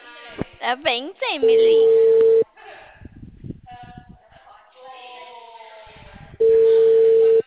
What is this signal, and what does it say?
Ringback tone (China, 450 Hz), with speech.